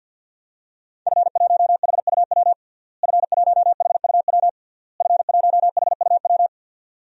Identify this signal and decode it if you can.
Morse code, heard as a keyed tone, about 47 wpm, 700 Hz. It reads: F1HUW F1HUW F1HUW
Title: DXCC-100 Call Signs - 5 Characters - Repeated 3x - 50wpm